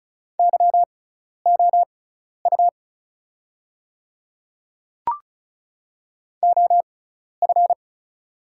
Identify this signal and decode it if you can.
Morse code, heard as a keyed tone, about 34 wpm, spaced as for 12 wpm, 700 Hz. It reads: YOU OF
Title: 6x Character Spacing - Top 200 Words - 35wpm